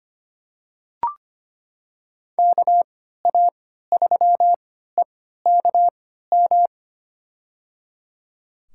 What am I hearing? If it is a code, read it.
Morse code, heard as a keyed tone, about 25 wpm, spaced as for 14 wpm, 700 Hz. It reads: KA3EKM
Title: USA Call Signs - 6 Characters - 3x Character Spacing - 25wpm